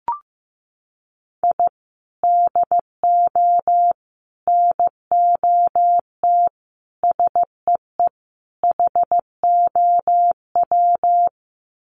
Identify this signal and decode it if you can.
Morse code, heard as a keyed tone, about 15 wpm, 700 Hz. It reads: I DO NOT SEE HOW
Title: Sets of 5 Words 15wpm